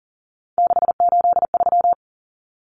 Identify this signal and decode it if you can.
Morse code, heard as a keyed tone, about 40 wpm, 700 Hz. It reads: 683
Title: Common Cut Numbers - 3-Digit 40wpm